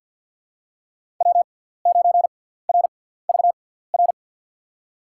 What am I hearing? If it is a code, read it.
Morse code, heard as a keyed tone, about 50 wpm, spaced as for 17 wpm, 700 Hz. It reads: W9RVR
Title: USA Call Signs - 5 Characters - 6x Character Spacing - 50wpm